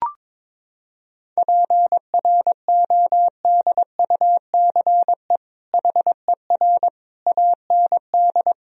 Morse code, heard as a keyed tone, about 22 wpm, 700 Hz. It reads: PRODUCE HER AND
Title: Sentences from Top 1000 Words - Review Part 3 - 22wpm